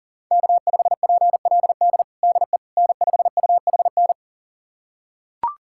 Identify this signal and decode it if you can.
Morse code, heard as a keyed tone, about 40 wpm, 700 Hz. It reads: K5PLD DE N5UHN
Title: USA Callsign DE Callsign - 5 Characters - 40wpm